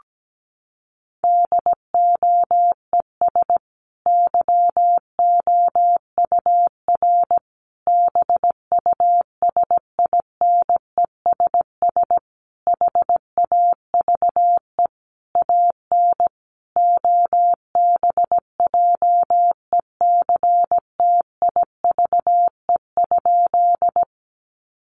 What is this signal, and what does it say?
Morse code, heard as a keyed tone, about 17 wpm, 700 Hz. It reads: DOES YOUR BUSINESS HAVE AN OBJECTIVE?
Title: Sentences from Top 1200 Words 17wpm